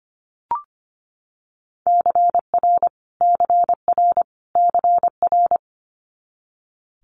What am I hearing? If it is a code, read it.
Morse code, heard as a keyed tone, about 25 wpm, 700 Hz. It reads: CR CR CR
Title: Multi-letter Phonemes Repeated 3x 25wpm